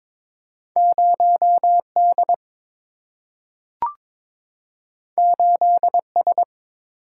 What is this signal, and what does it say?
Morse code, heard as a keyed tone, about 22 wpm, 700 Hz. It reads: 0D 8S
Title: Single-Letter Single-Number 22wpm